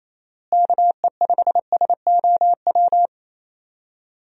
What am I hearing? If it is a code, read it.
Morse code, heard as a keyed tone, about 28 wpm, 700 Hz. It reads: KE5SOW
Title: USA Call Signs - 6 Characters - Repeated 3x - 28wpm